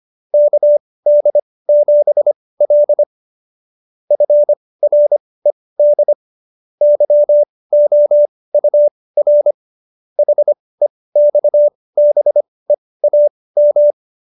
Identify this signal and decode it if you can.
Morse code, heard as a keyed tone, about 25 wpm, spaced as for 18 wpm, 590 Hz. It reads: KD7L FRED YOUR HEXBEAM